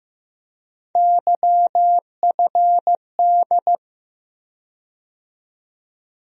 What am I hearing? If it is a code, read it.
Morse code, heard as a keyed tone, about 15 wpm, 700 Hz. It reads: YFD
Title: USA Call Sign Postfixes - 3 Letters 15wpm